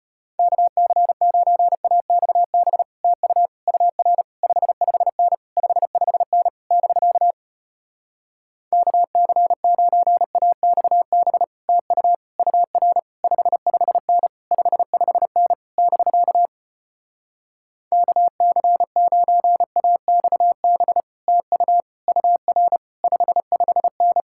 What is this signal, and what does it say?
Morse code, heard as a keyed tone, about 33 wpm, 700 Hz. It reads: KC9AXB TU UR 55N 55N <BK> KC9AXB TU UR 55N 55N <BK> KC9AXB TU UR 55N